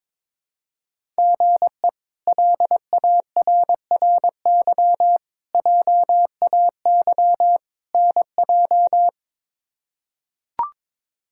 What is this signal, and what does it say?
Morse code, heard as a keyed tone, about 22 wpm, 700 Hz. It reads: GE LARRY JAY NJ
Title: K1USN SST QSO Practice - 22wpm